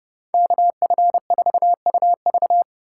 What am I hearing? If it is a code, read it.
Morse code, heard as a keyed tone, about 29 wpm, 700 Hz. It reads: KF4UV